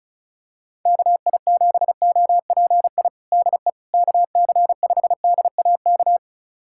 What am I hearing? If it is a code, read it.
Morse code, heard as a keyed tone, about 36 wpm, 700 Hz. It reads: KI7OPI DE KC5DAK
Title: USA Callsign DE Callsign 35wpm